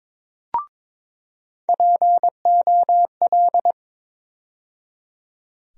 Morse code, heard as a keyed tone, about 22 wpm, 700 Hz. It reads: POL
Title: USA Call Sign Postfixes - 3 Letters 22wpm